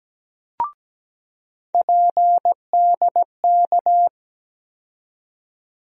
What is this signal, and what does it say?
Morse code, heard as a keyed tone, about 17 wpm, 700 Hz. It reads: PDK
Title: USA Call Sign Postfixes - 3 Letters 17wpm